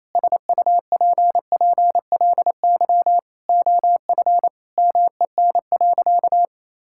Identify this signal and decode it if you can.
Morse code, heard as a keyed tone, about 28 wpm, 700 Hz. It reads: SUPPLY OF MEN.
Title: Sentences from Top 1000 Words - Review Part 5 - 28wpm